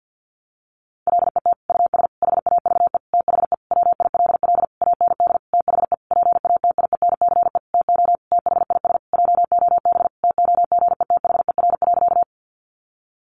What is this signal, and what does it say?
Morse code, heard as a keyed tone, about 50 wpm, 700 Hz. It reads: LET US HAVE THE WILL AND THE PATIENCE TO THIS JOB TOGETHER.
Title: Sentences from Top 1300 Words 50wpm